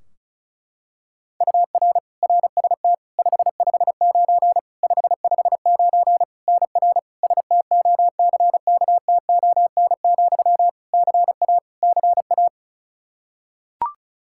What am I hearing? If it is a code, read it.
Morse code, heard as a keyed tone, about 36 wpm, 700 Hz. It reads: UR RST 559 559 NR STOCKTON, CA CA
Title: Practice QSO - Exchange 1 - Beginning - 35wpm